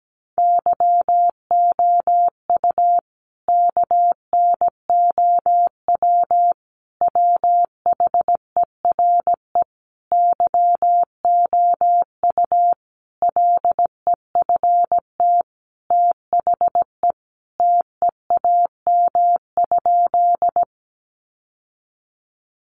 Morse code, heard as a keyed tone, about 17 wpm, 700 Hz. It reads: YOU KNOW WHERE YOU LEFT THE TEAM?
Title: Sentences from Top 1500 Words 17wpm